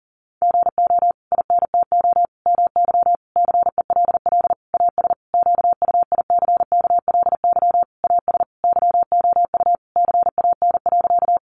Morse code, heard as a keyed tone, about 40 wpm, 700 Hz. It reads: GO INTO MY CELL AS QUICKLY AS YOU CAN.
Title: Sentences from Top 1100 Words 40wpm